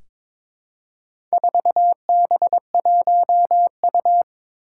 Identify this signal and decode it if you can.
Morse code, heard as a keyed tone, about 22 wpm, 700 Hz. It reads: 4B1U
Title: Four Challenging Letter-Number Combinations 22wpm